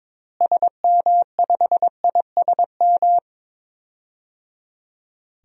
Morse code, heard as a keyed tone, about 22 wpm, 700 Hz. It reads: SM5ISM
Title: DXCC-100 Call Signs - 6 Characters - Repeated 3x - 22wpm